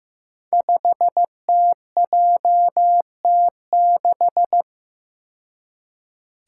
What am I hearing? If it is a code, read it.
Morse code, heard as a keyed tone, about 15 wpm, 700 Hz. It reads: 5TJT6
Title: Five Challenging Letter-Number Combinations 15wpm